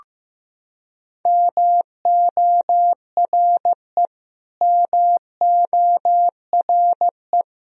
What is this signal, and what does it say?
Morse code, heard as a keyed tone, about 16 wpm, 700 Hz. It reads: MORE MORE
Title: Top 100 Words Repeated 15wpm